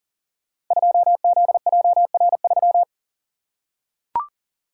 Morse code, heard as a keyed tone, about 42 wpm, 700 Hz. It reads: 2ZJR3
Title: Code Groups - 5 Letter:Numbers - 40wpm